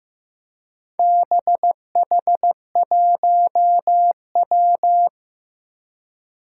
Morse code, heard as a keyed tone, about 15 wpm, 700 Hz. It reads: BH1W
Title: Four Challenging Letter-Number Combinations 15wpm